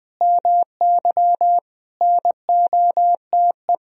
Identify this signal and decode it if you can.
Morse code, heard as a keyed tone, about 20 wpm, 700 Hz. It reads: MY NOTE
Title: Sentences from Top 1000 Words - Review Part 2 - 20wpm